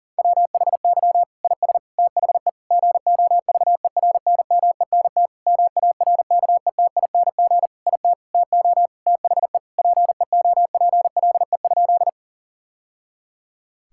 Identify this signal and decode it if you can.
Morse code, heard as a keyed tone, about 40 wpm, 700 Hz. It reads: WHY IS THE GOVERNMENT MARKETING IT TO THE PEOPLE?
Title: Sentences from Top 1000 Words - Review Part 5 - 40wpm